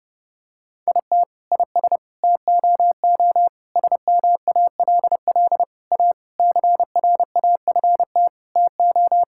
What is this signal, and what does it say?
Morse code, heard as a keyed tone, about 32 wpm, 700 Hz. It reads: IT IS TOO SMALL A CRAFT TO
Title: Sentences from Top 1000 Words - Review Part 6 - 30wpm